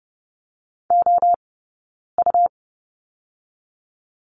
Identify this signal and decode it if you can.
Morse code, heard as a keyed tone, about 30 wpm, 700 Hz. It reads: O U
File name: Lesson 17d - Letter u - Two Characters 30wpm